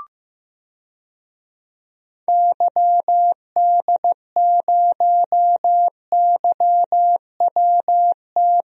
Morse code, heard as a keyed tone, about 15 wpm, 700 Hz. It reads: YD0YWT
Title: DXCC-100 Call Signs - 6 Characters - Repeated 3x - 15wpm